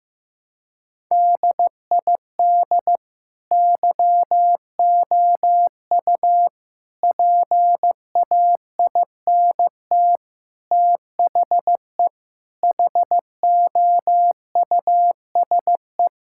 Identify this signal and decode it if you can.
Morse code, heard as a keyed tone, about 15 wpm, 700 Hz. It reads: DID YOU PAINT THE HOUSE
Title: Sentences from Top 1000 Words - Review Part 6 - 15wpm